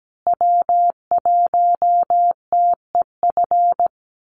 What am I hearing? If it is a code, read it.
Morse code, heard as a keyed tone, about 17 wpm, 700 Hz. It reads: W1TEF
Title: Call Signs Repeated 2x - 17wpm